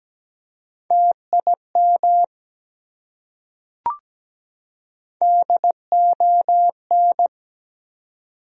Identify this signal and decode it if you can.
Morse code, heard as a keyed tone, about 17 wpm, 700 Hz. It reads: TIM DON
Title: USA Common Names - Max 3 Letters - 17wpm